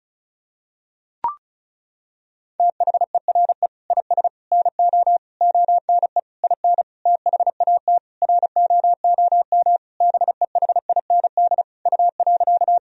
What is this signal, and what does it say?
Morse code, heard as a keyed tone, about 34 wpm, 700 Hz. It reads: THERE IS NO ONE IN THAT ROOM BEHIND U.
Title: Sentences from Top 500 Words and Abbreviations 35wpm